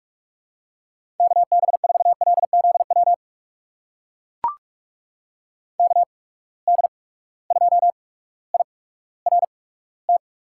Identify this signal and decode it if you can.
Morse code, heard as a keyed tone, about 44 wpm, spaced as for 19 wpm, 700 Hz. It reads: KB4LZW KD2IRT
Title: USA Call Signs - 6 Characters - 8x Character Spacing - 45wpm